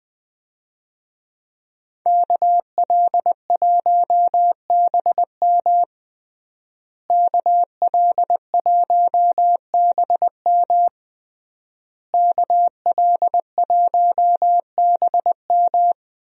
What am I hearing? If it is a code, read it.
Morse code, heard as a keyed tone, about 20 wpm, 700 Hz. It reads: KL1BM KL1BM KL1BM
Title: USA Call Signs - 5 Characters - Repeated 3x - 20wpm